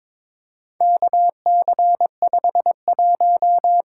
Morse code, heard as a keyed tone, about 22 wpm, 700 Hz. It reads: KC51
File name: practice / 22wpm / Four Letter-Number Combinations - 22wpm